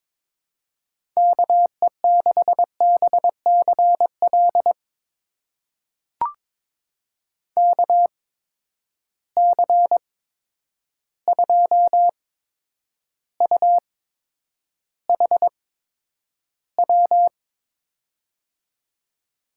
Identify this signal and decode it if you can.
Morse code, heard as a keyed tone, about 21 wpm, spaced as for 9 wpm, 700 Hz. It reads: KE6BCL KC2UHW
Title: USA Call Signs - 6 Characters - 8x Character Spacing - 22wpm